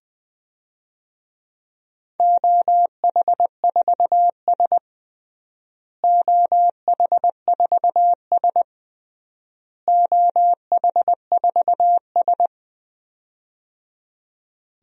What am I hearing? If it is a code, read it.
Morse code, heard as a keyed tone, about 20 wpm, 700 Hz. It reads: OH4S OH4S OH4S
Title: DXCC-100 Call Signs - 4 Characters - Repeated 3x - 20wpm